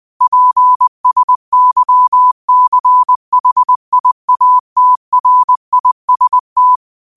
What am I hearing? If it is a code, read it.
Morse code, heard as a keyed tone, about 20 wpm, 1000 Hz. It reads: PSYCHIATRIST